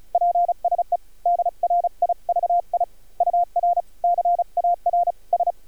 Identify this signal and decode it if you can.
Morse code, heard as a keyed tone, about 36 wpm, 700 Hz. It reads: PSE DRIVI UR CAR S
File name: archived / CRSnet